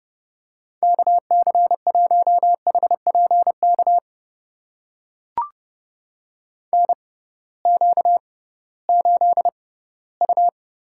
Morse code, heard as a keyed tone, about 30 wpm, spaced as for 17 wpm, 700 Hz. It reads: KC1HPK NQ8U
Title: USA Call Signs - 6x Character Spacing - 30wpm